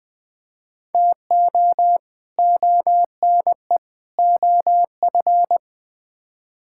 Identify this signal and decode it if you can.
Morse code, heard as a keyed tone, about 20 wpm, 700 Hz. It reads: TO ONE OF
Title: Build A Phrase - 4-Word Phrases - 20wpm